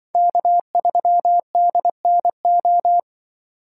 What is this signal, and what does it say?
Morse code, heard as a keyed tone, about 23 wpm, 700 Hz. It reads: K3DNO
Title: Speed Racing - USA 1x3 Call Signs 20wpm